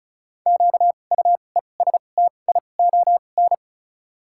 Morse code, heard as a keyed tone, about 36 wpm, spaced as for 25 wpm, 700 Hz. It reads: QUESTION
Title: 2x Character Spacing - Top 500 Words - 35wpm